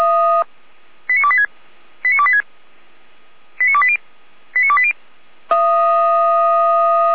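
Five-tone selective call (ZVEI), heard as quick runs of five tones, tones of about 70 ms. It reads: 89286 89286 89280 89280
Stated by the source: Five-tone ZVEI selective call 89223 received by SDR, served by Feuerwehr Orlamünde.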